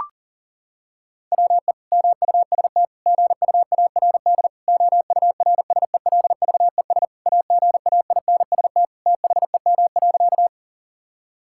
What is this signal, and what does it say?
Morse code, heard as a keyed tone, about 40 wpm, 700 Hz. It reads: WE MUST GUARD OURSELVES AGAINST THEM.
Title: Sentences from Top 1600 Words 40wpm